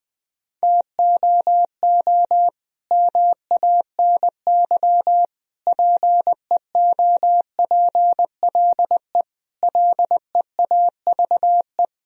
Morse code, heard as a keyed tone, about 20 wpm, 700 Hz. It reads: TOO MANY PEOPLE LEAVE
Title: Sentences from Top 1000 Words - Review Part 1 - 20wpm